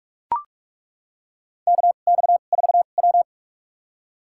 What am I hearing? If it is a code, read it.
Morse code, heard as a keyed tone, about 42 wpm, spaced as for 31 wpm, 700 Hz. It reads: KX4W
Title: USA Call Signs - 4 Characters - 2x Character Spacing - 45wpm